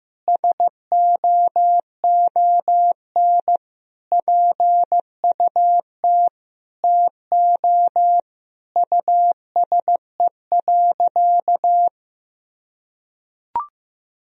Morse code, heard as a keyed tone, about 16 wpm, 700 Hz. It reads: SOON PUT TO USE.
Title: Sentences from Top 1000 Words - Review Part 3 - 15wpm